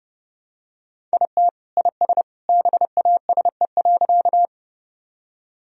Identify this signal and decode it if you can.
Morse code, heard as a keyed tone, about 30 wpm, 700 Hz. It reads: IT IS BASE.
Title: Sentences from Top 1000 Words - Review Part 1 - 30wpm